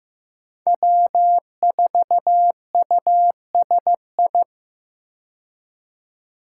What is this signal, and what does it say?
Morse code, heard as a keyed tone, about 15 wpm, 700 Hz. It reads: W4USI
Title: K1USN SST QSO Practice - 15wpm